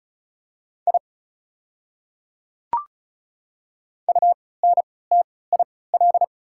Morse code, heard as a keyed tone, about 34 wpm, spaced as for 20 wpm, 700 Hz. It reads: I UNTIL
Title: 3x Character Spacing - Top 300 Words - 35wpm